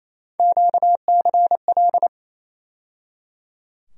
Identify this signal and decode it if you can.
Morse code, heard as a keyed tone, about 28 wpm, 700 Hz. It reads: QCL
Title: USA Call Sign Postfixes - 3 Letters 28wpm